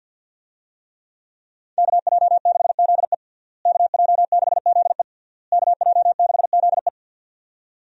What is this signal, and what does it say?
Morse code, heard as a keyed tone, about 51 wpm, 700 Hz. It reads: KJ6ZE KJ6ZE KJ6ZE
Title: USA Call Signs - 5 Characters - Repeated 3x - 50wpm